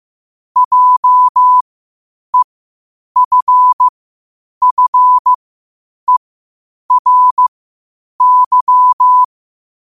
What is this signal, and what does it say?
Morse code, heard as a keyed tone, about 15 wpm, spaced as for 8 wpm, 1000 Hz. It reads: JEFFERY
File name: qotdfiles / 8wpm